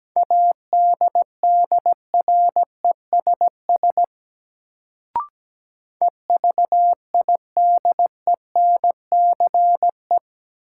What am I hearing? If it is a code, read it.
Morse code, heard as a keyed tone, about 17 wpm, 700 Hz. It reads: ADDRESS EVIDENCE
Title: Sets of 5 Words Encore - New Word Drill - 17wpm